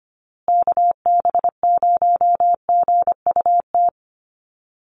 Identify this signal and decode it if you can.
Morse code, heard as a keyed tone, about 25 wpm, 700 Hz. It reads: KB0GUT
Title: USA Call Signs Repeated 2x 25wpm